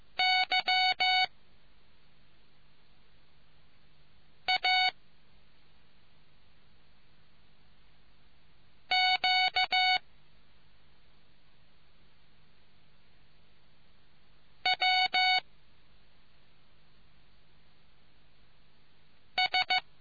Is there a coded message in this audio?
Yes: Morse code from 0.2 to 19.8 s, heard as a keyed tone, about 15 wpm, 740 Hz. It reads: Y A Q W S